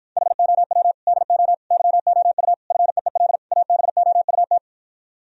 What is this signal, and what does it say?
Morse code, heard as a keyed tone, about 54 wpm, 700 Hz. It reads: HOW DO YOU FEEL ABOUT